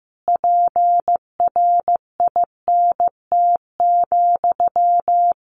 Morse code, heard as a keyed tone, about 15 wpm, 700 Hz. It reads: PRINT,
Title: A Sound Of Thunder 15wpm